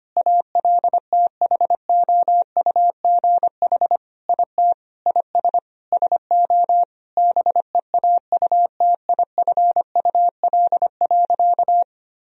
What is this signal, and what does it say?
Morse code, heard as a keyed tone, about 25 wpm, 700 Hz. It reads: ALTHOUGH IT IS SO BEAUTIFUL.
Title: Sentences from Top 1000 Words - Review Part 3 - 25wpm